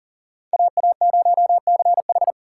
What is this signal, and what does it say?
Morse code, heard as a keyed tone, about 51 wpm, 700 Hz. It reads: AA0CH